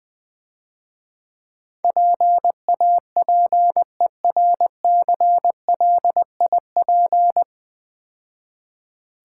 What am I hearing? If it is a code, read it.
Morse code, heard as a keyed tone, about 20 wpm, 700 Hz. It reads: PAPERCLIP
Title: Easy Words - Speed-Racing with Hints - 20wpm